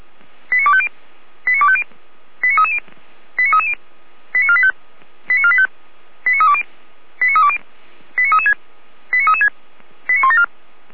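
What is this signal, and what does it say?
Five-tone selective call (ZVEI), heard as quick runs of five tones, tones of about 70 ms. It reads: 89250 89250 89330 89330 89485 89485 89320 89320 89336 89336 89174
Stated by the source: ZVEI five-tone selective call 89500 received on an SDR by the Feuerwehr Orlamünde fire brigade server.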